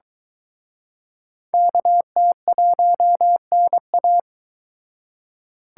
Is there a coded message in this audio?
Yes: Morse code from 1.5 to 4.2 s, heard as a keyed tone, about 24 wpm, 700 Hz. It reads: KT1NA